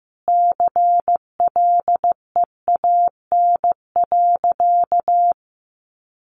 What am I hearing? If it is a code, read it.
Morse code, heard as a keyed tone, about 15 wpm, 700 Hz. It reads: CLEAN.
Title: Sentences from Top 1100 Words 15wpm